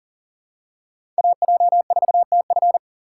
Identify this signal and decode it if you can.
Morse code, heard as a keyed tone, about 40 wpm, 690 Hz. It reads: AJ4TF
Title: K1USN SST QSO Practice - 40wpm